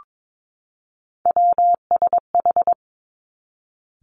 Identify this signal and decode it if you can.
Morse code, heard as a keyed tone, about 22 wpm, 700 Hz. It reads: WSH
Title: Three Challenging Letter-Number Combinations 22wpm